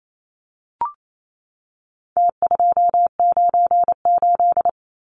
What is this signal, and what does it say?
Morse code, heard as a keyed tone, about 28 wpm, 700 Hz. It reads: T298
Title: Numbers - Contest Style - 4-Digit 28wpm